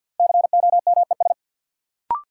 Morse code, heard as a keyed tone, about 48 wpm, 700 Hz. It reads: COMES
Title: Sentences from Top 1000 Words - Review Part 5 - New Word Drill - 50wpm